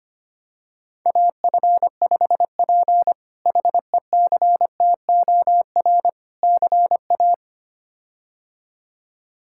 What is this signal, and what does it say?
Morse code, heard as a keyed tone, about 25 wpm, 700 Hz. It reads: AF5P HECTOR CA
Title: K1USN SST QSO Practice - 25wpm